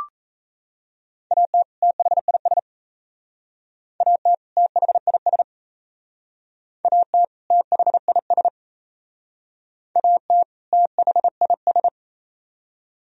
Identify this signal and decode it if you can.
Morse code, heard as a keyed tone, about 33 wpm, 700 Hz. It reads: AT THIS AT THIS AT THIS AT THIS
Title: Speed-Racing - Sets of 2 Words 28wpm